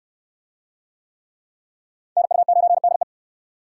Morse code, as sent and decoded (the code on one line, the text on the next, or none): -. ..- ----. -.. .
NU9DE